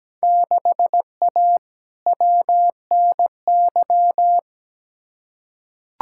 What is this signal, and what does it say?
Morse code, heard as a keyed tone, about 17 wpm, 700 Hz. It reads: 6A WNY
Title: Field Day Exchanges - 17wpm